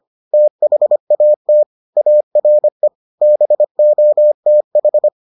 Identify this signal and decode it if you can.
Morse code, heard as a keyed tone, about 25 wpm, 600 Hz. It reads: THAT ARE BOTH